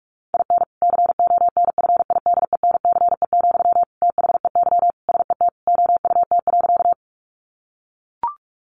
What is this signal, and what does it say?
Morse code, heard as a keyed tone, about 44 wpm, 700 Hz. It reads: IN CONFIDENCE, THEY SET OUT.
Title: Sentences from Top 1500 Words 45wpm